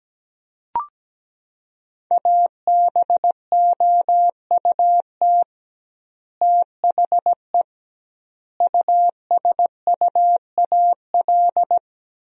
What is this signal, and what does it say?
Morse code, heard as a keyed tone, about 17 wpm, 700 Hz. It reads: ABOUT THE USUAL
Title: 2x Word-Spacing - Sentences from Top 400 Words - 17wpm